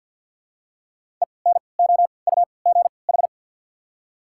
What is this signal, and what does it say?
Morse code, heard as a keyed tone, about 53 wpm, spaced as for 29 wpm, 700 Hz. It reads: ENOUGH
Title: 3x Character Spacing - Top 500 Words - 50wpm